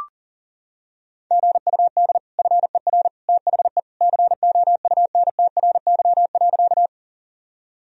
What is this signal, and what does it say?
Morse code, heard as a keyed tone, about 37 wpm, 700 Hz. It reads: GUD FER THE COUNTRY.
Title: Sentences from Top 300 Words and Abbreviations 40wpm